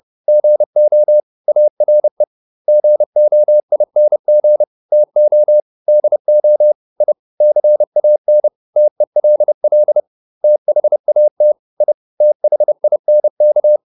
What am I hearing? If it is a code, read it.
Morse code, heard as a keyed tone, about 30 wpm, 600 Hz. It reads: GO ARE GOING TO DO I CAN TELL THAT I THINK